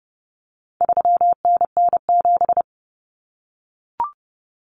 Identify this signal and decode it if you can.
Morse code, heard as a keyed tone, about 30 wpm, 700 Hz. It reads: 3NN7
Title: Common Cut Numbers - 4-Digit 30wpm